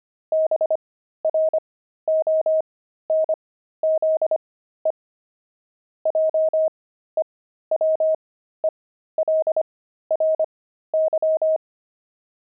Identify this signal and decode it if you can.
Morse code, heard as a keyed tone, about 25 wpm, spaced as for 13 wpm, 630 Hz. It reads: BRONZE JEWELRY
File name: pangram-fs-13-cs-25-deep